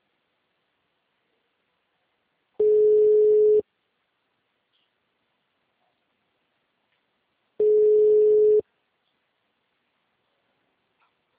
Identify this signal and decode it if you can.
Ringback tone (Europe, 425 Hz).